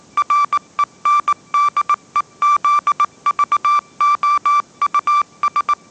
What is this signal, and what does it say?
Morse code, heard as a keyed tone, about 19 wpm, 1170 Hz. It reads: RENDEZVOUS
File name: 20wpm